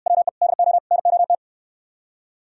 Morse code, heard as a keyed tone, about 66 wpm, 700 Hz. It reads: WE DO NOT